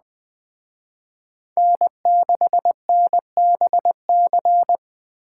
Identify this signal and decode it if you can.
Morse code, heard as a keyed tone, about 20 wpm, 700 Hz. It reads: N6NBC
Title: Speed Racing - USA Call Signs 17wpm